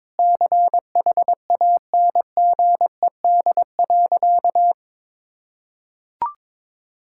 Morse code, heard as a keyed tone, about 22 wpm, 700 Hz. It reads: CHANGED.
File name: practice / 22wpm / Sentences with 3-Word Phrases - 22wpm